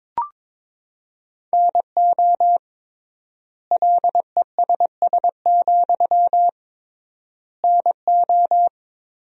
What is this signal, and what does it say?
Morse code, heard as a keyed tone, about 21 wpm, 700 Hz. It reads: NO LESS, NO
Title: 3x Word-Spacing - Sentences from Top 400 Words - 22wpm